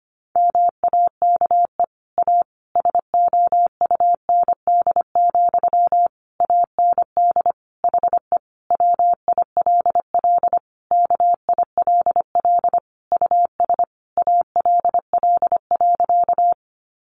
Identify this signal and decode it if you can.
Morse code, heard as a keyed tone, about 25 wpm, 700 Hz. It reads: MAKE A SOUND, AND HE WILL KILL US ALL.
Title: Sentences from Top 1400 Words 25wpm